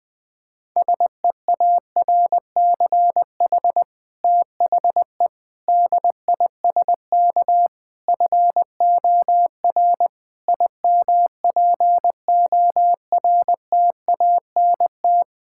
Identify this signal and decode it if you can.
Morse code, heard as a keyed tone, about 20 wpm, 700 Hz. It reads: SEARCH THE DISK FOR IMPORTANT
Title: Sentences from Top 1000 Words - Review Part 6 - 20wpm